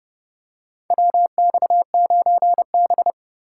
Morse code, heard as a keyed tone, about 30 wpm, 700 Hz. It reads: WX9B
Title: USA Call Signs - 2x Character Spacing - 30wpm